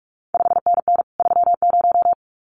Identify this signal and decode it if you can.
Morse code, heard as a keyed tone, about 46 wpm, 700 Hz. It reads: 5NN 30
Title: CQ WW DX Contest Exchange Encore - 45wpm